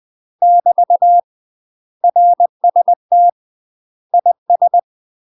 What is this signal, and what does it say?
Morse code, heard as a keyed tone, about 20 wpm, 700 Hz. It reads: <BT> RST IS
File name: UK MASTER QSOs TRAINING / 20WPM / UK MASTER QSOs 0010